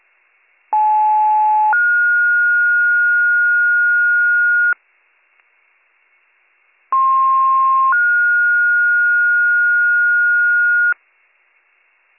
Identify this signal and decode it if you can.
Two-tone page, heard as a one-second tone, then a three-second tone, about 860 then 1430 Hz, about 1060 then 1430 Hz. It is heard from 0.7 to 10.9 s.